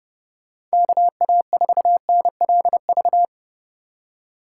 Morse code, heard as a keyed tone, about 34 wpm, 700 Hz. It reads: KA4NLV